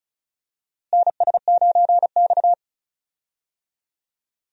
Morse code, heard as a keyed tone, about 34 wpm, 700 Hz. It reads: NS9X
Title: USA Call Signs - 4 Characters - Repeated 2x - 35wpm